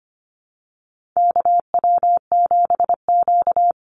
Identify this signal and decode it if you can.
Morse code, heard as a keyed tone, about 25 wpm, 700 Hz. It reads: KW7Q
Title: Call Signs Repeated 3x - 25wpm